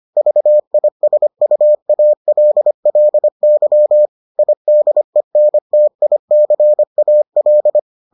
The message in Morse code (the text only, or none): VISUALLY IDENTICAL